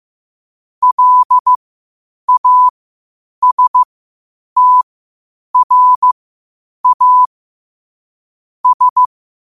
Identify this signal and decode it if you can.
Morse code, heard as a keyed tone, about 15 wpm, spaced as for 8 wpm, 1000 Hz. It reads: LASTRA S
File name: qotdfiles / 8wpm